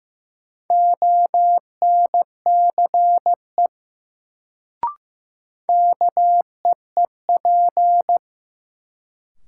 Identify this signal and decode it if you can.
Morse code, heard as a keyed tone, about 15 wpm, 700 Hz. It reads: ONCE KEEP
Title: Sentences from Top 300 Words - New Word Drill - 15wpm